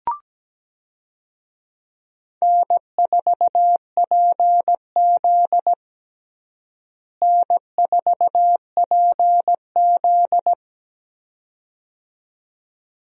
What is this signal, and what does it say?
Morse code, heard as a keyed tone, about 17 wpm, 700 Hz. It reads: N4PZ N4PZ
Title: USA Call Signs - 4 Characters - Repeated 2x - 17wpm